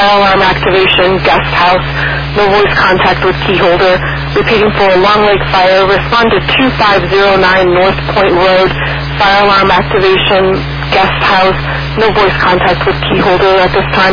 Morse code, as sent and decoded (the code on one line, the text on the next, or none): none